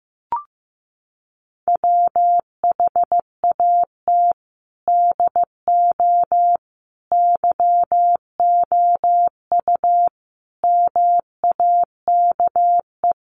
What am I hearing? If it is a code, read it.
Morse code, heard as a keyed tone, about 15 wpm, 700 Hz. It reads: WHAT DO YOU MAKE
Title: Sets of 5 Words 15wpm